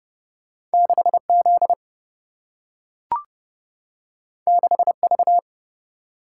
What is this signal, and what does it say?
Morse code, heard as a keyed tone, about 30 wpm, 700 Hz. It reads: 6 Z 6 V
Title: Single-Letter Single-Number 30wpm